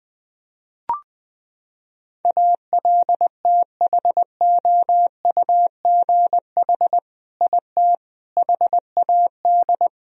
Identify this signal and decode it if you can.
Morse code, heard as a keyed tone, about 20 wpm, 700 Hz. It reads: ALTHOUGH IT HAD
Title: Sentences from Top 1000 Words - Review Part 3 - 20wpm